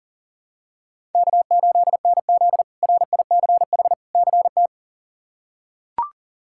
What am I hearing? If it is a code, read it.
Morse code, heard as a keyed tone, about 40 wpm, 700 Hz. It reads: K8NZ RICH CT
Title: K1USN SST QSO Practice - 40wpm